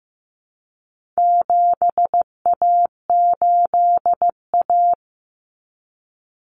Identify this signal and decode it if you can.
Morse code, heard as a keyed tone, about 15 wpm, 700 Hz. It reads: 7A8A
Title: Common Cut Numbers - 4-Digit 15wpm